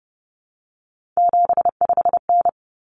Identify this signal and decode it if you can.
Morse code, heard as a keyed tone, about 30 wpm, 700 Hz. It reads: 75N
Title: Common Cut Numbers - 3-Digit 30wpm